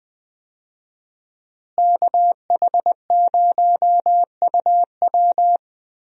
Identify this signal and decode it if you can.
Morse code, heard as a keyed tone, about 20 wpm, 700 Hz. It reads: KH0UW
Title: USA Call Signs - 5 Characters - Repeated 3x - 20wpm